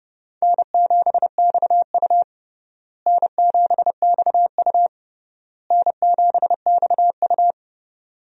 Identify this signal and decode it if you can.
Morse code, heard as a keyed tone, about 30 wpm, 700 Hz. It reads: N7XU N7XU N7XU
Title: USA Call Signs - 4 Characters - Repeated 3x - 30wpm